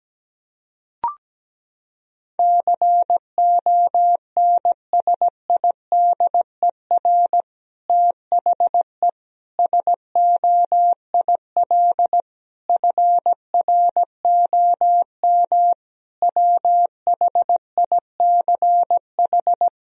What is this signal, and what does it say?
Morse code, heard as a keyed tone, about 17 wpm, 700 Hz. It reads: CONSIDER THE SOIL FROM WHICH
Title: Sentences from Top 1000 Words - Review Part 4 - 17wpm